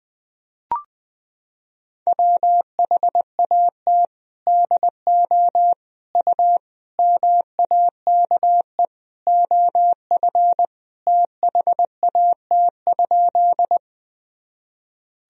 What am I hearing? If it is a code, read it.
Morse code, heard as a keyed tone, about 20 wpm, 700 Hz. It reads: WHAT DO U MAKE OF THAT?
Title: Sentences from Top 100 Words and Abbreviations 20wpm